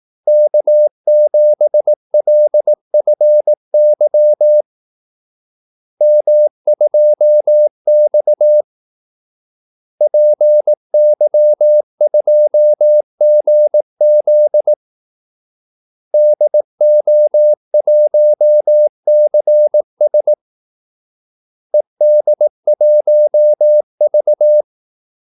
Morse code, heard as a keyed tone, about 18 wpm, 600 Hz. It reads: K7LFY M2X PY2GZ DO1CS ED1V
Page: Kaldesignaler 18 wpm